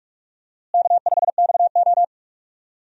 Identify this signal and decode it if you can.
Morse code, heard as a keyed tone, about 44 wpm, 700 Hz. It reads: K5XO
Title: USA Call Signs - 4 Characters - Repeated 2x - 45wpm